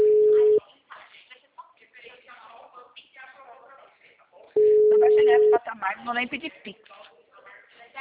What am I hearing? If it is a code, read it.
Ringback tone (Europe, 425 Hz), with speech.
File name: out- -2006- - - .79